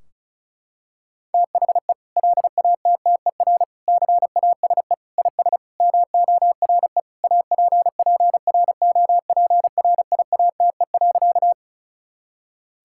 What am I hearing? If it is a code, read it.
Morse code, heard as a keyed tone, about 34 wpm, 700 Hz. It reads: THE LATTER CASE IS MORE APPROPRIATE.
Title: Sentences from Top 1600 Words 35wpm